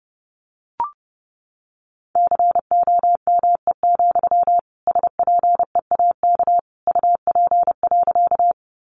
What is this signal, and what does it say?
Morse code, heard as a keyed tone, about 29 wpm, 700 Hz. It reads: COME, SPEAK UP.
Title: Sentences from Top 1100 Words 30wpm